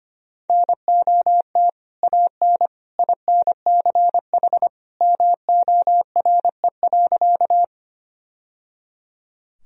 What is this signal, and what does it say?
Morse code, heard as a keyed tone, about 25 wpm, 700 Hz. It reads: NOT AN INCH MORE.
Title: Sentences from Top 1000 Words - Review Part 3 - 25wpm